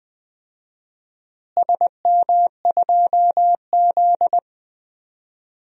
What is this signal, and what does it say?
Morse code, heard as a keyed tone, about 20 wpm, 700 Hz. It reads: SM2Z
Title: DXCC-100 Call Signs - 4 Characters - Repeated 3x - 20wpm